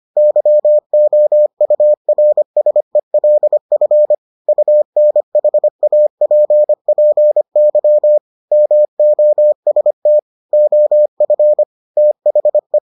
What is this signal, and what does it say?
Morse code, heard as a keyed tone, about 25 wpm, 600 Hz. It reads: YOURSELF UNHAPPY MOST OF THE